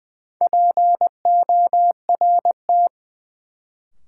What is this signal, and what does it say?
Morse code, heard as a keyed tone, about 20 wpm, 700 Hz. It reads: PORT
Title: Sentences from Top 1000 Words - Review Part 1 - New Word Drill - 20wpm